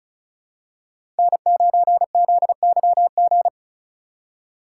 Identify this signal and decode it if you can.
Morse code, heard as a keyed tone, about 36 wpm, 700 Hz. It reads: N9ZYG
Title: USA Call Signs - 2x Character Spacing - 35wpm